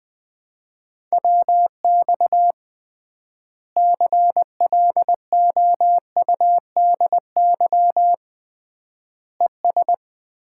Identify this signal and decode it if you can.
Morse code, heard as a keyed tone, about 21 wpm, 700 Hz. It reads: WX CLOUDY ES
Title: Common QSO Phrases - 3x Word-Spacing - 20wpm